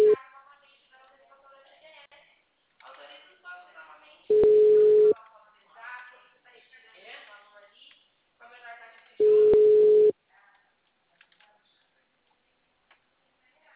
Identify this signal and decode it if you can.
Ringback tone (Europe, 425 Hz), with speech.